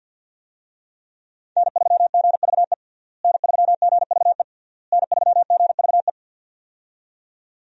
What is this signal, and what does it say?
Morse code, heard as a keyed tone, about 52 wpm, 700 Hz. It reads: N3GVE N3GVE N3GVE
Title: USA Call Signs - 5 Characters - Repeated 3x - 50wpm